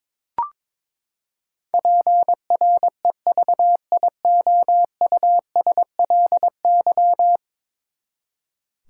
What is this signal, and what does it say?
Morse code, heard as a keyed tone, about 22 wpm, 700 Hz. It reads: PREVIOUSLY
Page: morse-code-ninja-practice-22wpm